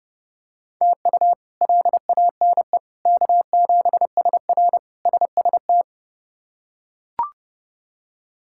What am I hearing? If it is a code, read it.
Morse code, heard as a keyed tone, about 31 wpm, 700 Hz. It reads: TU LANE K7SR SST
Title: K1USN SST QSO Practice - 30wpm